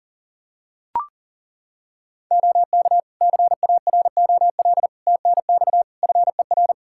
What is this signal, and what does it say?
Morse code, heard as a keyed tone, about 37 wpm, 700 Hz. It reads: OK CAROL TNX FER
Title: Practice QSO - Exchange 4 - Ending - 40wpm